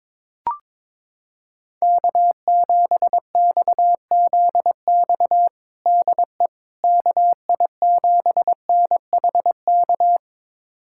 Morse code, heard as a keyed tone, about 21 wpm, 700 Hz. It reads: K7XZX DE KI7NHK
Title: USA Callsign DE Callsign 22wpm